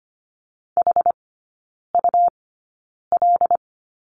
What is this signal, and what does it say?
Morse code, heard as a keyed tone, about 25 wpm, spaced as for 9 wpm, 700 Hz. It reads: HUL
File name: Lesson 17e - Letter u - Three Characters 25wpm